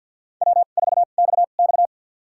Morse code, heard as a keyed tone, about 47 wpm, spaced as for 35 wpm, 700 Hz. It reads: W4XX
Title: USA Call Signs - 4 Characters - 2x Character Spacing - 50wpm